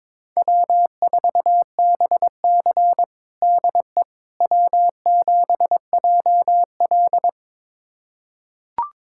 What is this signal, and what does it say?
Morse code, heard as a keyed tone, about 22 wpm, 700 Hz. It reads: W4BC DE W7JL
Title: USA Callsign DE Callsign - 4 Characters - 22wpm